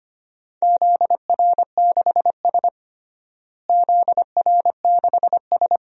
Morse code, heard as a keyed tone, about 25 wpm, 700 Hz. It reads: ZR6S ZR6S
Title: DXCC-100 Call Signs - 4 Characters - Repeated 2x - 25wpm